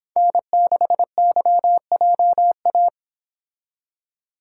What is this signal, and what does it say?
Morse code, heard as a keyed tone, about 29 wpm, 700 Hz. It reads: N6YJA